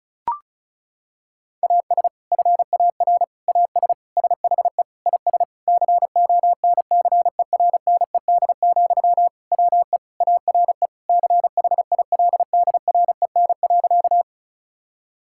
Morse code, heard as a keyed tone, about 36 wpm, 700 Hz. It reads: AS FAR AS SHE IS CONCERNED, WE ARE CHILDREN.
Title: Sentences from Top 1000 Words - Review Part 6 - 35wpm